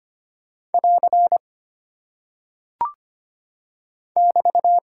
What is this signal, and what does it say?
Morse code, heard as a keyed tone, about 25 wpm, 700 Hz. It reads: <AR> <BT>
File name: Single Punctuation-Prosign - 25wpm